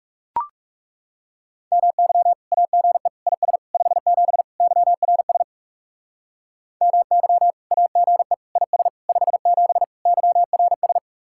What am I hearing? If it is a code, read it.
Morse code, heard as a keyed tone, about 37 wpm, 700 Hz. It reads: MY AGE IS 57 YRS MY AGE IS 57 YRS
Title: Speed-Racing Common QSO Phrases Encore - 30wpm